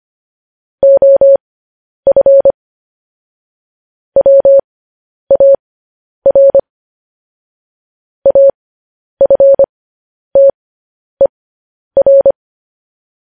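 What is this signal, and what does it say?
Morse code, heard as a keyed tone, about 25 wpm, spaced as for 10 wpm, 550 Hz. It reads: OF WAR AFTER
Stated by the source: Morse code news headlines.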